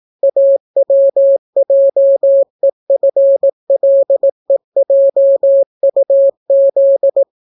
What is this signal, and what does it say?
Morse code, heard as a keyed tone, about 18 wpm, 550 Hz. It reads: AWJEFLEJUZ